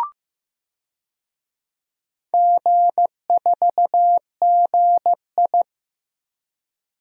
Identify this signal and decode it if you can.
Morse code, heard as a keyed tone, about 15 wpm, 700 Hz. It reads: G4GI
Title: DXCC-100 Call Signs - 4 Characters - Repeated 3x - 15wpm